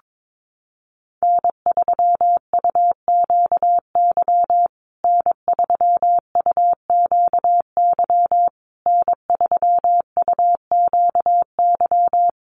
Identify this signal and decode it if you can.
Morse code, heard as a keyed tone, about 22 wpm, 700 Hz. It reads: N3UQY N3UQY N3UQY
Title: USA Call Signs Repeated 3x 22wpm